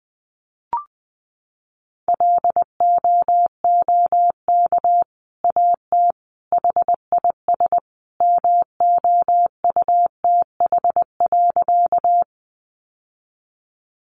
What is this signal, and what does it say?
Morse code, heard as a keyed tone, about 20 wpm, 700 Hz. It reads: LOOK AT HIS MOUTH.
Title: Sentences from Top 1500 Words 20wpm